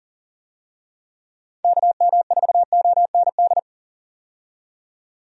Morse code, heard as a keyed tone, about 40 wpm, 700 Hz. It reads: KM4OND
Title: USA Call Signs - 6 Characters - 40wpm